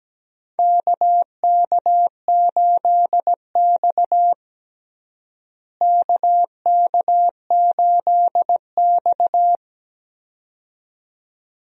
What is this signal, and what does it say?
Morse code, heard as a keyed tone, about 17 wpm, 700 Hz. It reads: KK8X KK8X
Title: USA Call Signs - 4 Characters - Repeated 2x - 17wpm